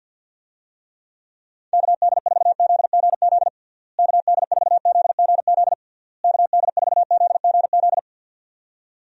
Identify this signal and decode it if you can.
Morse code, heard as a keyed tone, about 50 wpm, 700 Hz. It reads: KD4ZGZ KD4ZGZ KD4ZGZ
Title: USA Call Signs - 6 Characters - Repeated 3x - 50wpm